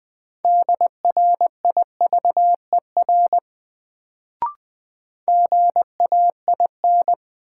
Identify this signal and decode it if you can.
Morse code, heard as a keyed tone, about 21 wpm, 700 Hz. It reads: DRIVER GAIN
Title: Sentences from Top 1000 Words - New Word Drill - 20wpm